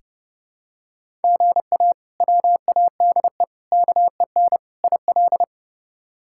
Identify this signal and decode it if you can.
Morse code, heard as a keyed tone, about 30 wpm, 700 Hz. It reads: GA WADE KEN IL
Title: K1USN SST QSO Practice - 30wpm